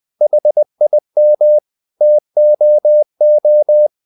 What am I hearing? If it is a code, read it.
Morse code, heard as a keyed tone, about 20 wpm, 600 Hz. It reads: HIM TOO
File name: the old man and the sea part 2 20wpm